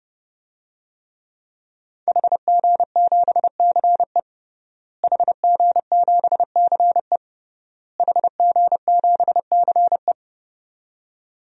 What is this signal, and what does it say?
Morse code, heard as a keyed tone, about 30 wpm, 700 Hz. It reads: HG7CE HG7CE HG7CE
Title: DXCC-100 Call Signs - 5 Characters - Repeated 3x - 30wpm